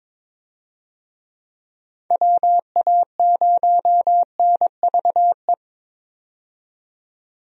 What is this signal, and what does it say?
Morse code, heard as a keyed tone, about 22 wpm, 700 Hz. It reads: WA0NVE
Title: USA Call Signs - 6 Characters - 22wpm